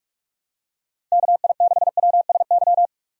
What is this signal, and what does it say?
Morse code, heard as a keyed tone, about 46 wpm, 700 Hz. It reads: KI6WSY
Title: USA Call Signs - 6 Characters - 45wpm